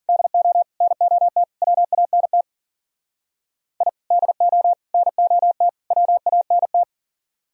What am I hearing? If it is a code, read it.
Morse code, heard as a keyed tone, about 51 wpm, 700 Hz. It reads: DO NOT WANT I DO NOT WANT